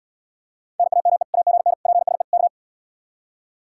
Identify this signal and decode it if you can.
Morse code, heard as a keyed tone, about 80 wpm, 700 Hz. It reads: NAME AGN PSE K